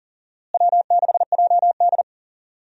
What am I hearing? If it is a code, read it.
Morse code, heard as a keyed tone, about 37 wpm, 700 Hz. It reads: W6JD
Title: USA Call Signs - 4 Characters - Repeated 3x - 40wpm